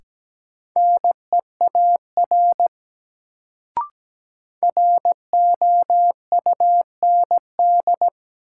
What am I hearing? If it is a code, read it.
Morse code, heard as a keyed tone, about 17 wpm, 700 Hz. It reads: NEAR ROUND
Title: Sentences from Top 200 Words - New Word Drill - 17wpm